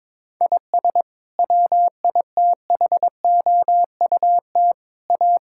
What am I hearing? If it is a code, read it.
Morse code, heard as a keyed tone, about 22 wpm, 700 Hz. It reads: IS WITHOUT A
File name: practice / 22wpm / Sets of 7 Words - 22wpm